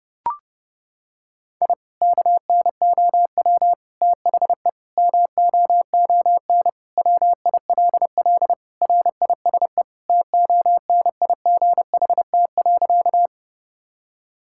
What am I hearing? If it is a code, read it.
Morse code, heard as a keyed tone, about 32 wpm, 700 Hz. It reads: I KNOW THE MOON WILL RISE TONIGHT.
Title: Sentences from Top 1000 Words - Review Part 6 - 30wpm